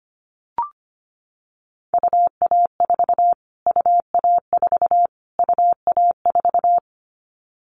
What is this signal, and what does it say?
Morse code, heard as a keyed tone, about 25 wpm, 700 Hz. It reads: UA4 UA4 UA4
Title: Callsign Prefixes for Top 25 DXCC - Repeated 3x - 25wpm